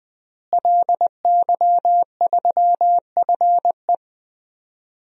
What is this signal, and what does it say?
Morse code, heard as a keyed tone, about 20 wpm, 700 Hz. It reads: LY3FE
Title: DXCC-100 Call Signs - 5 Characters - Repeated 3x - 20wpm